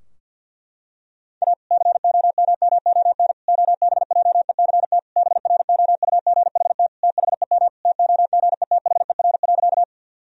Morse code, heard as a keyed tone, about 51 wpm, 700 Hz. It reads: A COMMON OBJECT BROUGHT THEM TOGETHER.
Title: Sentences from Top 1000 Words - Review Part 3 - 50wpm